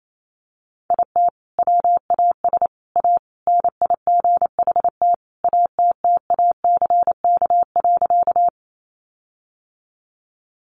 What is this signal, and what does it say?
Morse code, heard as a keyed tone, about 28 wpm, 700 Hz. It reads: IT WAS A NIGHT ATTACK.
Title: Sentences from Top 1500 Words 28wpm